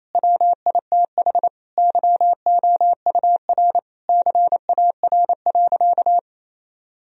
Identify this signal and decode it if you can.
Morse code, heard as a keyed tone, about 27 wpm, 700 Hz. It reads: WITH YOUR CAR.
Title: Sentences from Top 1000 Words - Review Part 1 - 28wpm